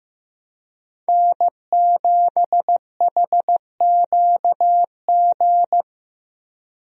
Morse code, heard as a keyed tone, about 15 wpm, 700 Hz. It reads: N7HQG
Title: USA Call Signs - 5 Characters - Repeated 3x - 15wpm